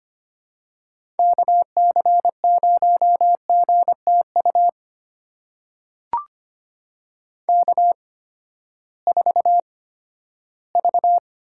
Morse code, heard as a keyed tone, about 25 wpm, spaced as for 14 wpm, 700 Hz. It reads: KC0GTU K4V
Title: USA Call Signs - 8x Character Spacing - 25wpm